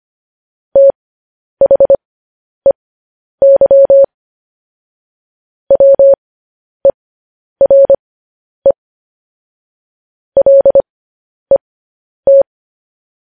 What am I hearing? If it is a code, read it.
Morse code, heard as a keyed tone, about 25 wpm, spaced as for 10 wpm, 550 Hz. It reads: THEY WERE LET
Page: news headlines in morse code at 10 wpm